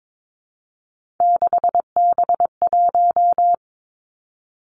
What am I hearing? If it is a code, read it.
Morse code, heard as a keyed tone, about 22 wpm, 700 Hz. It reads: 6B1
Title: Three Challenging Letter-Number Combinations 22wpm